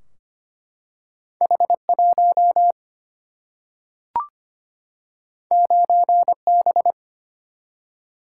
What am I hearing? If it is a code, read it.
Morse code, heard as a keyed tone, about 25 wpm, 700 Hz. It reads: H 1 9 B
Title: Single-Letter Single-Number 25wpm